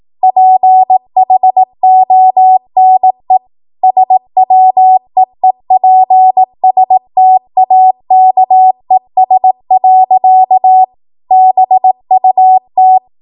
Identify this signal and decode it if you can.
Morse code, heard as a keyed tone, about 18 wpm, 750 Hz. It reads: PHONE SWEEPSTAKES. BUT
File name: Archive / 18 WPM / 18WPM